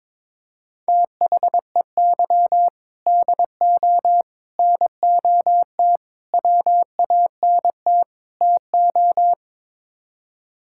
Morse code, heard as a keyed tone, about 22 wpm, 700 Hz. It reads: THEY DO NOT WANT TO
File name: practice / 22wpm / Build A Phrase - 5-Word Phrases - 22wpm